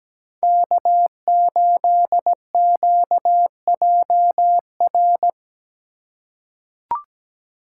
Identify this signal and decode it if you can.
Morse code, heard as a keyed tone, about 17 wpm, 700 Hz. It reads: K8QJR
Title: USA Call Signs - 5 Characters - 8x Character Spacing - 17wpm